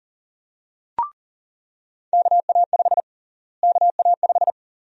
Morse code, heard as a keyed tone, about 40 wpm, 700 Hz. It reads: KA5 KA5
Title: USA Call Sign Prefixes - Repeated 2x - 40wpm